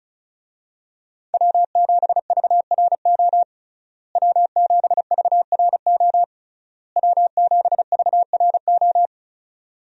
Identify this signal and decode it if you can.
Morse code, heard as a keyed tone, about 36 wpm, 700 Hz. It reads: W7VRO W7VRO W7VRO
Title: USA Call Signs - 5 Characters - Repeated 3x - 35wpm